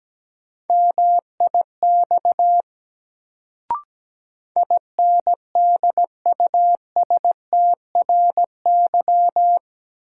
Morse code, heard as a keyed tone, about 17 wpm, 700 Hz. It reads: MIX INDUSTRY
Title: Sentences from Top 1000 Words - Review Part 4 - New Word Drill - 17wpm